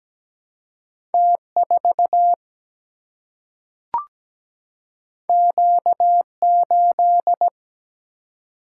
Morse code, heard as a keyed tone, about 17 wpm, 700 Hz. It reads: T4 Q8
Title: Single-Letter Single-Number 17wpm